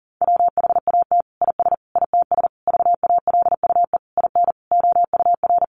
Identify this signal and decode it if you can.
Morse code, heard as a keyed tone, about 37 wpm, 700 Hz. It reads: WHAT IS ITS VALUE IN OUR
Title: Sentences from Top 600 Words 40wpm